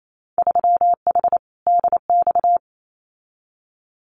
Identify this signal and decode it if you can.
Morse code, heard as a keyed tone, about 28 wpm, 700 Hz. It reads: 3H DX
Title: Winter Field Day Exchanges 28wpm